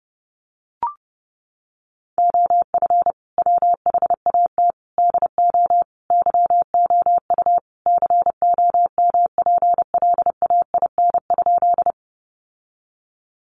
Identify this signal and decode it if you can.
Morse code, heard as a keyed tone, about 30 wpm, 700 Hz. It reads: OF WHAT DO YOU COMPLAIN?
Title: Sentences from Top 1300 Words 30wpm